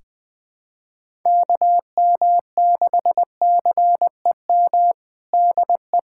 Morse code, heard as a keyed tone, about 21 wpm, 700 Hz. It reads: KM6CEM DE
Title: USA Callsign DE Callsign - 6 Characters - 20wpm